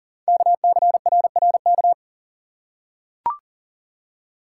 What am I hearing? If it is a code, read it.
Morse code, heard as a keyed tone, about 42 wpm, 700 Hz. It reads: KCRRK
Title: Code Groups - 5 Letter:Numbers - 40wpm